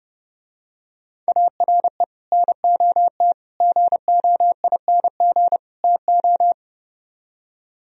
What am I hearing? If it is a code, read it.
Morse code, heard as a keyed tone, about 30 wpm, 700 Hz. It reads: ARE NOT GOING TO